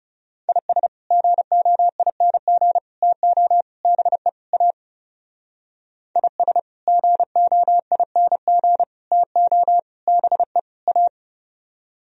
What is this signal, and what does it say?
Morse code, heard as a keyed tone, about 37 wpm, 700 Hz. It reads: IS GOING TO BE A IS GOING TO BE A
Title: Speed-Racing - Sets of 5 Words 30wpm